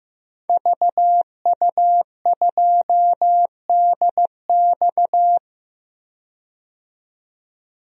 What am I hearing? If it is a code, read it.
Morse code, heard as a keyed tone, about 16 wpm, 700 Hz. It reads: VU2DX
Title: DXCC-100 Call Signs - 5 Characters - 15wpm